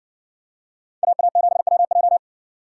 Morse code, heard as a keyed tone, about 57 wpm, 700 Hz. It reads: AA7WJ